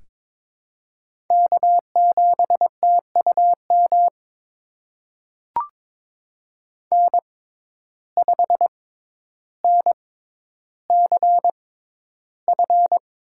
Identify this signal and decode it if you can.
Morse code, heard as a keyed tone, about 21 wpm, spaced as for 11 wpm, 700 Hz. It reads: K7TUM N5NCF
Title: USA Call Signs - 5 Characters - 6x Character Spacing - 22wpm